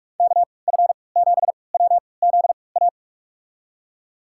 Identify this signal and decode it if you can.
Morse code, heard as a keyed tone, about 46 wpm, spaced as for 26 wpm, 700 Hz. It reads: KF7WZA
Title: USA Call Signs - 6 Characters - 3x Character Spacing - 45wpm